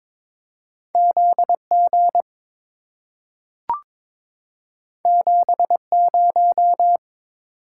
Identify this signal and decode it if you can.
Morse code, heard as a keyed tone, about 22 wpm, 700 Hz. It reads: ZG 70
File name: practice / 22wpm / Two Letter-Number Combinations - 22wpm